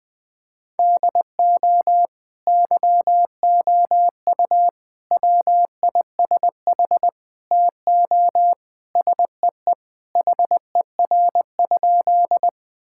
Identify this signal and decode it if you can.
Morse code, heard as a keyed tone, about 21 wpm, 700 Hz. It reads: DO YOU WISH TO SEE HER?
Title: Sentences from Top 1600 Words 20wpm